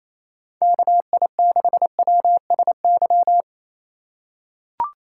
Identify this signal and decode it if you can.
Morse code, heard as a keyed tone, about 28 wpm, 700 Hz. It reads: KI6WSY
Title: USA Call Signs - 6 Characters - 3x Character Spacing - 28wpm